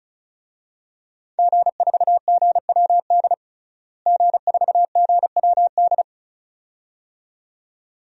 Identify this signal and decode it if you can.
Morse code, heard as a keyed tone, about 34 wpm, 700 Hz. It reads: G4GWD G4GWD
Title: DXCC-100 Call Signs - 5 Characters - Repeated 2x - 35wpm